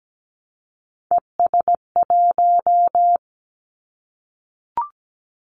Morse code, heard as a keyed tone, about 17 wpm, 700 Hz. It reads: ES1
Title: Callsign Prefixes for Top 100 DXCC - 17wpm